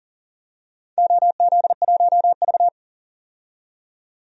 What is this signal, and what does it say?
Morse code, heard as a keyed tone, about 40 wpm, 700 Hz. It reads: OZ1V
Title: DXCC-100 Call Signs - 4 Characters - 40wpm